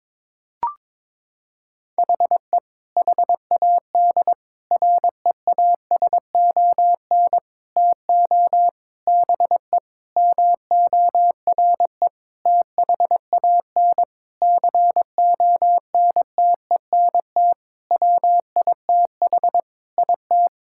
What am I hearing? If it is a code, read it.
Morse code, heard as a keyed tone, about 22 wpm, 700 Hz. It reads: HE HAD REASON TO BE MORE THAN CONTENT WITH IT
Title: Sentences from Top 1000 Words - Review Part 6 - 22wpm